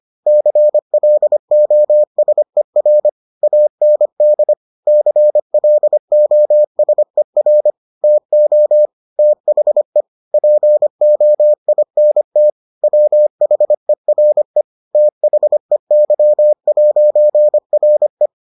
Morse code, heard as a keyed tone, about 25 wpm, 600 Hz. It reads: CLOSER AND CLOSER TO THE POINT WHERE THEY'RE